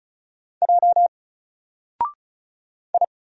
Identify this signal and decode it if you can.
Morse code, heard as a keyed tone, about 35 wpm, 700 Hz. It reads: J I
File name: Single-Letter-Number-35wpm